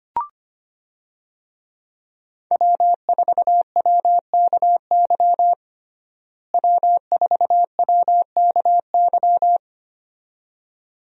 Morse code, heard as a keyed tone, about 25 wpm, 700 Hz. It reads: W4WKY W4WKY
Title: USA Call Signs - 5 Characters - Repeated 2x - 25wpm